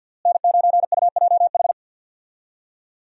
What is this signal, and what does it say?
Morse code, heard as a keyed tone, about 46 wpm, 700 Hz. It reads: N9UJH
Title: USA Call Signs - 5 Characters - Repeated 3x - 50wpm